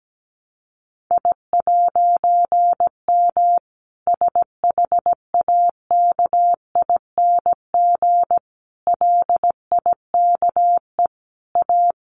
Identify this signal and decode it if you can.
Morse code, heard as a keyed tone, about 17 wpm, 700 Hz. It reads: I'M SHAKING LIKE A
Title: A Sound Of Thunder 17wpm